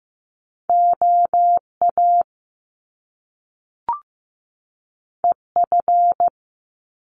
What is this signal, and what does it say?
Morse code, heard as a keyed tone, about 15 wpm, 700 Hz. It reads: OA EF
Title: Top Two Letter Combinations in Words 15wpm